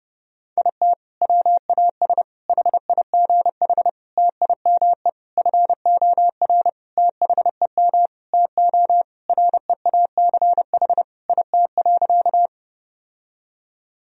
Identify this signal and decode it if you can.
Morse code, heard as a keyed tone, about 32 wpm, 700 Hz. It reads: IT WAS HIGH TIME FOR THEM TO REACH IT.
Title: Sentences from Top 1000 Words - Review Part 2 - 30wpm